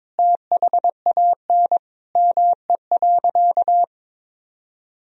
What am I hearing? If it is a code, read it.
Morse code, heard as a keyed tone, about 21 wpm, 700 Hz. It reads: THAN ME.
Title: Sentences from Top 400 Words and Abbreviations 22wpm